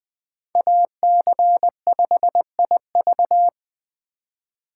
Morse code, heard as a keyed tone, about 20 wpm, 700 Hz. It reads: AC5IV
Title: USA Call Signs - 3x Character Spacing - 20wpm